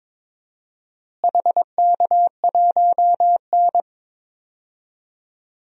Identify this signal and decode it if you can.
Morse code, heard as a keyed tone, about 22 wpm, 700 Hz. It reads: HK1N
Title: DXCC-100 Call Signs - 4 Characters - Repeated 2x - 22wpm